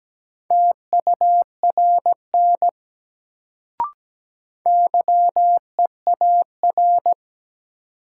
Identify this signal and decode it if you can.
Morse code, heard as a keyed tone, about 17 wpm, 700 Hz. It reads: TURN YEAR
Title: Sentences from Top 200 Words - New Word Drill - 17wpm